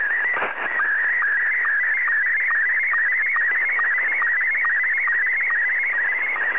SSTV (slow-scan television), Scottie 1.